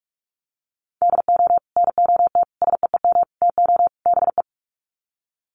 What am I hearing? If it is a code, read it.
Morse code, heard as a keyed tone, about 44 wpm, 700 Hz. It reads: DO NOT SEEM TO BE
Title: Sets of 5 Words Encore2 45wpm